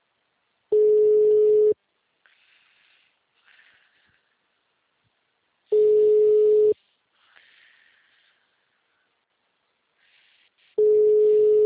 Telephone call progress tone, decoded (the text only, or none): ringback tone